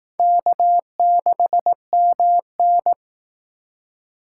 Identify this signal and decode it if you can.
Morse code, heard as a keyed tone, about 18 wpm, 700 Hz. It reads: K6MN
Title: Speed Racing - USA 1x2 Call Signs 15wpm